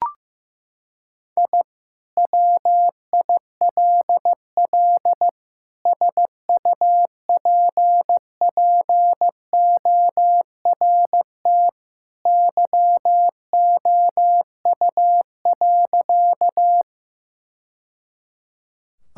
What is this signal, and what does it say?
Morse code, heard as a keyed tone, about 15 wpm, 700 Hz. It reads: I WILL SUPPORT YOU.
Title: Sentences from Top 1000 Words - Review Part 5 - 15wpm